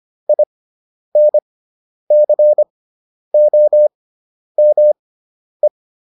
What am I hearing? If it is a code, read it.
Morse code, heard as a keyed tone, about 25 wpm, spaced as for 10 wpm, 600 Hz. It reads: INCOME